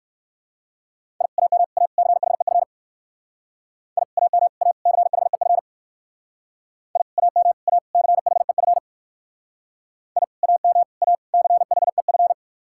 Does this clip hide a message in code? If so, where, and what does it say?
Yes: Morse code from 1.2 to 12.3 s, heard as a keyed tone, about 58 wpm, 700 Hz. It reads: I AM A CHEF I AM A CHEF I AM A CHEF I AM A CHEF